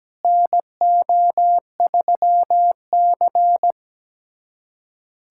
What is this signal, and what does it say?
Morse code, heard as a keyed tone, about 17 wpm, 700 Hz. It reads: NO3C
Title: USA Callsign DE Callsign - 4 Characters - 17wpm